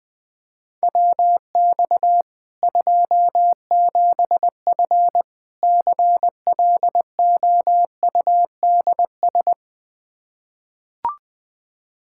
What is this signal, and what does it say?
Morse code, heard as a keyed tone, about 20 wpm, 700 Hz. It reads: WX 27F CLOUDS
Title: K1USN SST QSO Practice 20wpm